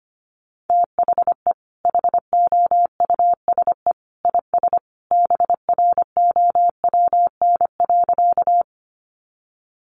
Morse code, heard as a keyed tone, about 25 wpm, 700 Hz. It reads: THE HOUSE IS BROWN.
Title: Sentences from Top 1200 Words 25wpm